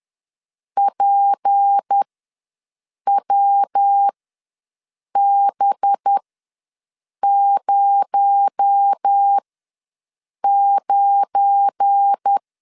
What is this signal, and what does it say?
Morse code, heard as a keyed tone, about 10 wpm, spaced as for 6 wpm, 800 Hz. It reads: PWB09
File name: LEZIONI / 9 286 ws 10wpm 4032